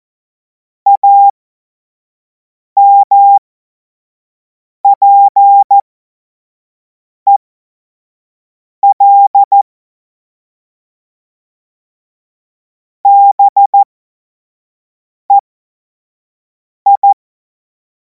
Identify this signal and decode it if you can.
Morse code, heard as a keyed tone, about 14 wpm, spaced as for 5 wpm, 800 Hz. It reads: AMPEL BEI
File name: Morsekurs / Witze / Witz 07. Juli 2026 25 F70